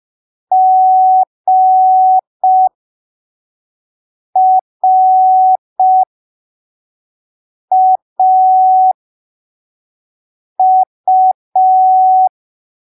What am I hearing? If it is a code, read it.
Morse code, heard as a keyed tone, about 5 wpm, spaced as for 3 wpm, 750 Hz. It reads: GRAU
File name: arquivos / texto / texto 11 05ppm